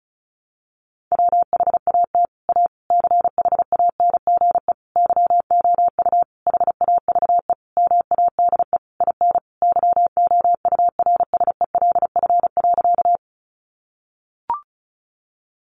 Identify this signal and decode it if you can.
Morse code, heard as a keyed tone, about 34 wpm, 700 Hz. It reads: WHAT A CHANGE YOU HAVE MADE IN YOURSELF.
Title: Sentences from Top 600 Words 35wpm